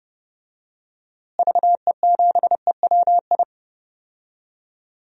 Morse code, heard as a keyed tone, about 30 wpm, 700 Hz. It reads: VE7EWI
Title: DXCC-100 Call Signs - 6 Characters - Repeated 2x - 30wpm